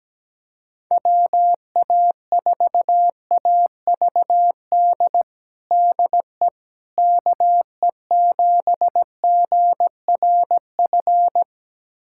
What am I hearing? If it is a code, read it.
Morse code, heard as a keyed tone, about 17 wpm, 700 Hz. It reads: WA4AVD DE KE7GRF
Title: USA Callsign DE Callsign 17wpm